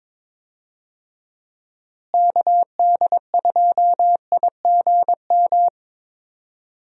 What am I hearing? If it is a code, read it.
Morse code, heard as a keyed tone, about 22 wpm, 700 Hz. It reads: KD2IGM
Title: USA Call Signs - 6 Characters - Repeated 2x - 22wpm